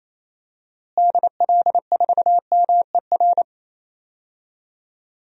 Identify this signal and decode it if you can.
Morse code, heard as a keyed tone, about 28 wpm, 700 Hz. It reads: DL4MER
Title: DXCC-100 Call Signs - 6 Characters - Repeated 2x - 28wpm